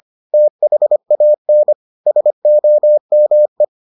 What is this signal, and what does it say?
Morse code, heard as a keyed tone, about 25 wpm, 600 Hz. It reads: THAN SOME